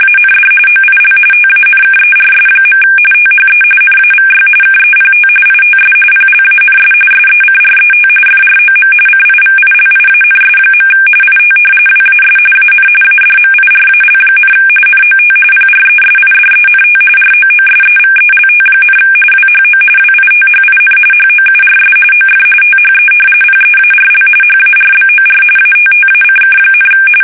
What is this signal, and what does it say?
RTTY, about 45.45 baud. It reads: AND OIL TANKERS. THE NAVY KEPT ITS PLANS SECRET FROM CIVILIANS. IT WAS AN ADMIRAL, NOT PRESIDENT ROOSEVELT, WHO WITHIN HOURS OF THE PEARL HARBOR ATTACK,
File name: fsk850